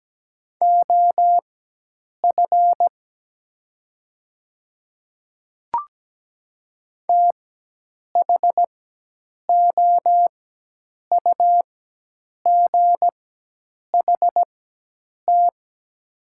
Morse code, heard as a keyed tone, about 17 wpm, spaced as for 8 wpm, 690 Hz. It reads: OF THOUGHT
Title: 4x Character Spacing - Top 300 Words - 17wpm